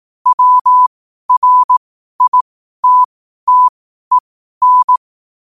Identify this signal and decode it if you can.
Morse code, heard as a keyed tone, about 18 wpm, spaced as for 12 wpm, 1000 Hz. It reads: WRITTEN
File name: qotdfiles / 12wpm